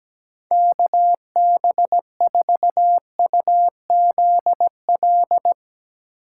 Morse code, heard as a keyed tone, about 17 wpm, 700 Hz. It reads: KB4UZL
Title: USA Call Signs - 3x Character Spacing - 17wpm